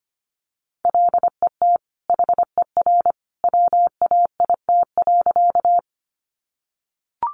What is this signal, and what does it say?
Morse code, heard as a keyed tone, about 25 wpm, 700 Hz. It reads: LET HER WAIT.
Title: Sentences from Top 500 Words 25wpm